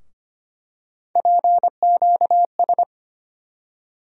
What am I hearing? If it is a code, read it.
Morse code, heard as a keyed tone, about 25 wpm, 700 Hz. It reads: PQS
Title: USA Call Sign Postfixes - 3 Letters 25wpm